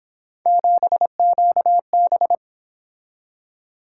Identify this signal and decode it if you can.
Morse code, heard as a keyed tone, about 26 wpm, 700 Hz. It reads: 7QB
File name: Speed-Racing - Three Letter-Number Combinations - 22wpm